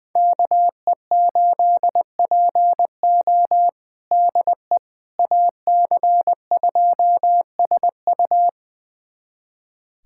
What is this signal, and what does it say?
Morse code, heard as a keyed tone, about 20 wpm, 700 Hz. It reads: KE8PO DE AC2SU
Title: USA Callsign DE Callsign - 5 Characters - 20wpm